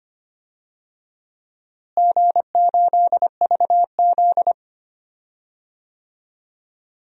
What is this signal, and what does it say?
Morse code, heard as a keyed tone, about 25 wpm, 700 Hz. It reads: G8VZ
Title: DXCC-100 Call Signs - 4 Characters - Repeated 3x - 25wpm